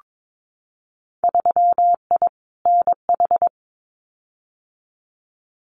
Morse code, heard as a keyed tone, about 22 wpm, 700 Hz. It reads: 3I NH
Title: Winter Field Day Exchanges Encore 22wpm